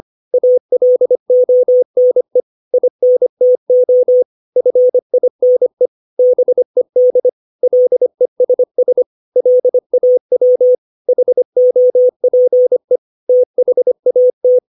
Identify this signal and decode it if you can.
Morse code, heard as a keyed tone, about 25 wpm, 500 Hz. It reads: ALONE INTO FINE BED LESS LAW HOPE THAT
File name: 632 words / 632 Words 25WPM